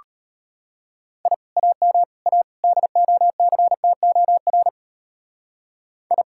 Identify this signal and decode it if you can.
Morse code, heard as a keyed tone, about 29 wpm, 700 Hz. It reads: I AM A DOCTOR I
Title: Speed-Racing Common QSO Phrases - 25wpm